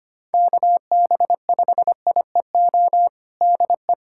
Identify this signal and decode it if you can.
Morse code, heard as a keyed tone, about 25 wpm, 700 Hz. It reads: KB5IEO DE
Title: USA Callsign DE Callsign 25wpm